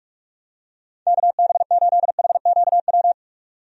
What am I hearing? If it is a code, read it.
Morse code, heard as a keyed tone, about 46 wpm, 700 Hz. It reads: KB8HQW
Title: USA Call Signs - 6 Characters - 45wpm